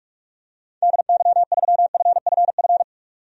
Morse code, heard as a keyed tone, about 47 wpm, 690 Hz. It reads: DY3FFF
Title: DXCC-100 Call Signs - 6 Characters - 45wpm